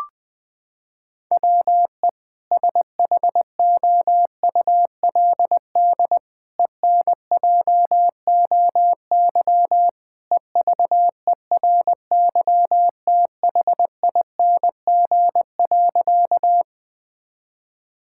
Morse code, heard as a keyed tone, about 21 wpm, 700 Hz. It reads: WE SHOULD ENJOY EVERYTHING.
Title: Sentences from Top 1000 Words - Review Part 5 - 20wpm